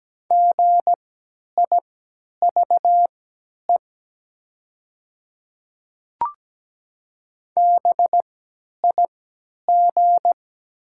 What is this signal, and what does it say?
Morse code, heard as a keyed tone, about 17 wpm, spaced as for 10 wpm, 700 Hz. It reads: GIVE BIG
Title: 3x Character Spacing - Top 200 Words - 17wpm